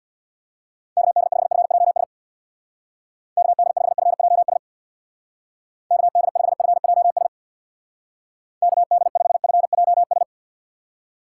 Morse code, heard as a keyed tone, about 64 wpm, 700 Hz. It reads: KD5FJS KD5FJS KD5FJS KD5FJS